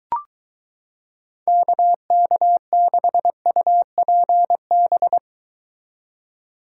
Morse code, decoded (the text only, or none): KK6UPB